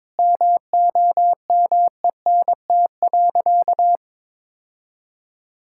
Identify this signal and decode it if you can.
Morse code, heard as a keyed tone, about 22 wpm, 700 Hz. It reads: MOMENT.
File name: practice / 22wpm / Sentences with 3-Word Phrases - 22wpm